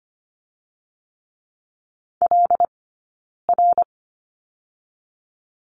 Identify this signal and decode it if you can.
Morse code, heard as a keyed tone, about 25 wpm, 700 Hz. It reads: L R
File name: Lesson 15d - Number 5 - Two Characters 25wpm